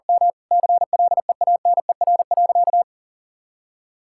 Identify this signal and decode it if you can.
Morse code, heard as a keyed tone, about 40 wpm, 700 Hz. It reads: M CLEANER.
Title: Pangram Challenge v2 - 40wpm